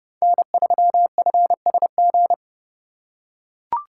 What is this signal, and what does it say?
Morse code, heard as a keyed tone, about 31 wpm, 700 Hz. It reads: N3FSG
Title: USA Call Signs - 5 Characters - 4x Character Spacing - 30wpm